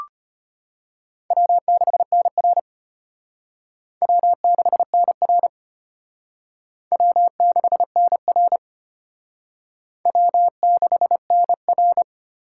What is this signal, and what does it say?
Morse code, heard as a keyed tone, about 30 wpm, 700 Hz. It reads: W6NR W6NR W6NR W6NR
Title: Speed Racing - USA 1x2 Call Signs 25wpm